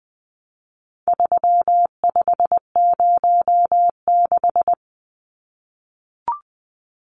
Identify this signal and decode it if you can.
Morse code, heard as a keyed tone, about 20 wpm, 700 Hz. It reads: 3506
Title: Numbers - 4-Digit 20wpm